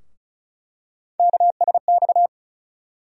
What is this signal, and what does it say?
Morse code, heard as a keyed tone, about 34 wpm, 700 Hz. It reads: KSX
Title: USA Call Sign Postfixes - 3 Letters 35wpm